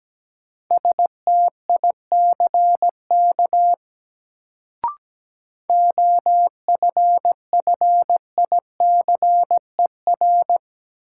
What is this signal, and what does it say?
Morse code, heard as a keyed tone, about 17 wpm, 700 Hz. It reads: STICK OFFICER
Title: Sentences from Top 1200 Words - New Word Drill - 17wpm